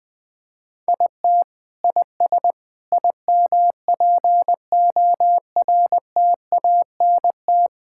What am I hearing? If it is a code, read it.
Morse code, heard as a keyed tone, about 20 wpm, 700 Hz. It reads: IT IS IMPORTANT
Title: Sentences with 5-Word Phrases - Spoken Beginning - 20wpm